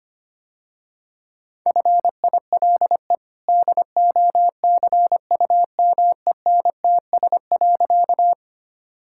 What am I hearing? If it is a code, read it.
Morse code, heard as a keyed tone, about 25 wpm, 690 Hz. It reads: FILE DOCUMENTS.
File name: Easy Sentences with Hints - 25wpm